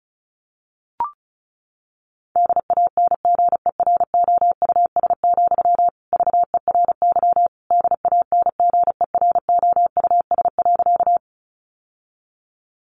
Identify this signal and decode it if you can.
Morse code, heard as a keyed tone, about 34 wpm, 700 Hz. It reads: DANGEROUS, VERY DANGEROUS.
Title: Sentences from Top 1200 Words 35wpm